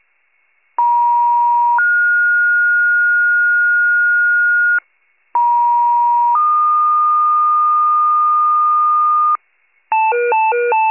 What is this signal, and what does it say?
Two-tone page, heard as a one-second tone, then a three-second tone, about 950 then 1430 Hz, about 950 then 1220 Hz, with noise. It is heard from 0.8 to 9.4 s.